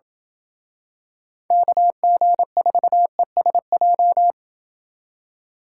Morse code, heard as a keyed tone, about 30 wpm, 700 Hz. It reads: KG4ESJ